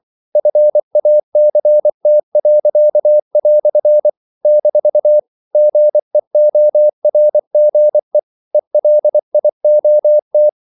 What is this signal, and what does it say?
Morse code, heard as a keyed tone, about 24 wpm, 600 Hz. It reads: FACT." - GEORGE ELIOT
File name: CW / Practice / EnglishQuotations0224